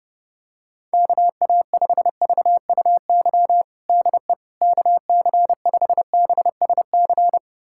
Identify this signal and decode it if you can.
Morse code, heard as a keyed tone, about 30 wpm, 700 Hz. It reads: KA5VUY DE KC5BSC
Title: USA Callsign DE Callsign - 6 Characters - 30wpm